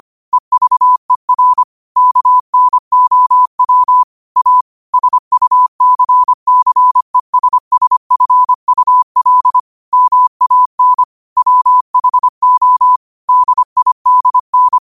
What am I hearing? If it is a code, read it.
Morse code, heard as a keyed tone, about 25 wpm, 1000 Hz. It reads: EVER KNOW A SUCCESSFUL MAN WHO DIDN
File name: qotdfiles / 25wpm